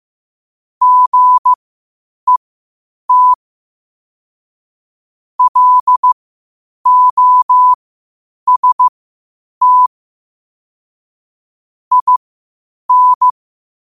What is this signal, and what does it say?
Morse code, heard as a keyed tone, about 15 wpm, spaced as for 8 wpm, 1000 Hz. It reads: GET LOST IN
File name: qotdfiles / 8wpm